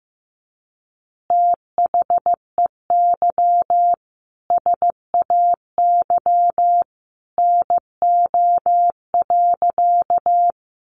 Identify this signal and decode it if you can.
Morse code, heard as a keyed tone, about 15 wpm, 700 Hz. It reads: THEY SAY NO.
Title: Easy Phrases 15wpm